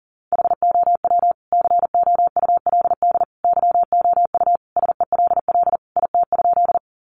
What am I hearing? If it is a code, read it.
Morse code, heard as a keyed tone, about 40 wpm, 700 Hz. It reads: HOW COULD YOU SELL IT?
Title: Sentences from Top 1100 Words 40wpm